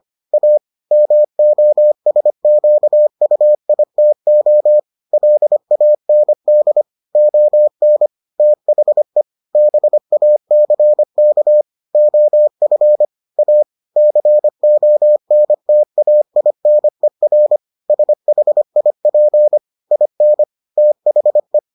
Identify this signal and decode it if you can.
Morse code, heard as a keyed tone, about 25 wpm, 600 Hz. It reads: A MOSQUITO LAND ON THE BACK OF A CONTAINER SHIP IN THE